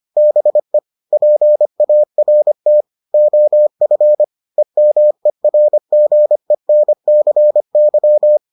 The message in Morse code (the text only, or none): BE PART OF EMERGENCY